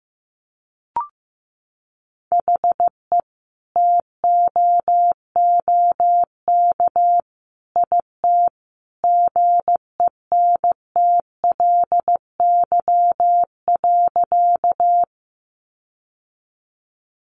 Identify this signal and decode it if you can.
Morse code, heard as a keyed tone, about 15 wpm, 700 Hz. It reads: HE TOOK IT GENTLY.
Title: Sentences from Top 1500 Words 15wpm